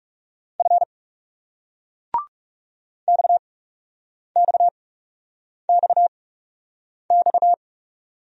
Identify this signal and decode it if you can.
Morse code, heard as a keyed tone, about 36 wpm, spaced as for 8 wpm, 700 Hz. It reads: F XXXX
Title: Speed-Racing Single Letters 30wpm